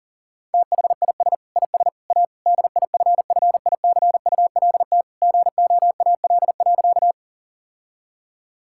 Morse code, heard as a keyed tone, about 40 wpm, 700 Hz. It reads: THIS IS A DIFFICULT GOAL.
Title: Sentences from Top 1000 Words - Review Part 5 - 40wpm